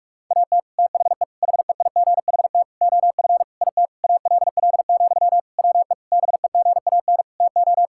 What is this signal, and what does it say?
Morse code, heard as a keyed tone, about 41 wpm, 700 Hz. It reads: AT THE HEIGHT OF IT ALL, WE BEGAN TO
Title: Sentences from Top 1600 Words 45wpm